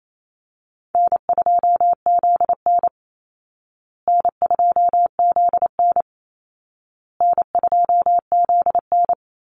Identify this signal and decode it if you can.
Morse code, heard as a keyed tone, about 28 wpm, 700 Hz. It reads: N2ZN N2ZN N2ZN
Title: Easy Call Signs Repeated 3x - 28wpm